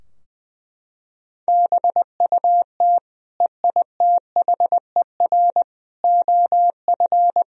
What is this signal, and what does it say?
Morse code, heard as a keyed tone, about 20 wpm, 700 Hz. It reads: BUT EITHER OF
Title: Sentences from Top 1000 Words - Review Part 4 - 20wpm